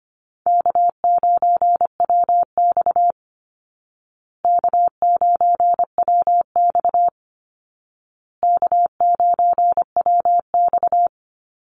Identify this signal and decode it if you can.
Morse code, heard as a keyed tone, about 25 wpm, 700 Hz. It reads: K9WX K9WX K9WX
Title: Easy Call Signs Repeated 3x - 25wpm